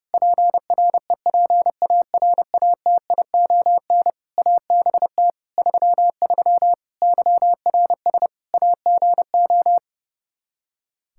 Morse code, heard as a keyed tone, about 32 wpm, 700 Hz. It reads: PREPARATION ABT 33 YRS AGO
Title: K1USN SST QSO Practice 30wpm